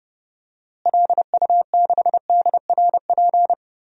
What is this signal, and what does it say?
Morse code, heard as a keyed tone, about 31 wpm, 700 Hz. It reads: LU6DRP
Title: DXCC-100 Call Signs - 6 Characters - Repeated 3x - 30wpm